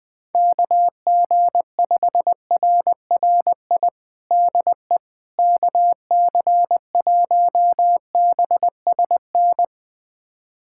Morse code, heard as a keyed tone, about 20 wpm, 700 Hz. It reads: KG5RRI DE KC1BSN
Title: USA Callsign DE Callsign 20wpm